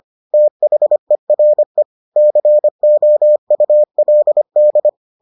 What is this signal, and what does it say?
Morse code, heard as a keyed tone, about 25 wpm, 600 Hz. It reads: THERE COULD